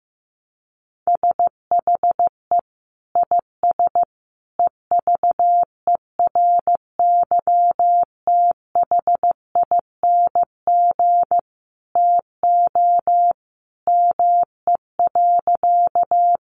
Morse code, heard as a keyed tone, about 15 wpm, 700 Hz. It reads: SHE IS EVERYTHING TO ME.
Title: Sentences from Top 700 Words 15wpm